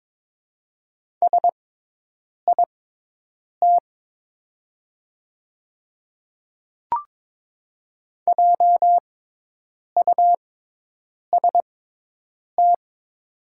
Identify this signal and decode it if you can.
Morse code, heard as a keyed tone, about 22 wpm, spaced as for 8 wpm, 700 Hz. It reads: SIT JUST
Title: 6x Character Spacing - Top 500 Words - 22wpm